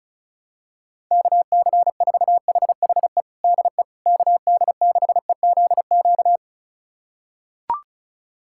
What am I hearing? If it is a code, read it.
Morse code, heard as a keyed tone, about 36 wpm, 690 Hz. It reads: KC4HHE DE KD6EZQ
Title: USA Callsign DE Callsign - 6 Characters - 35wpm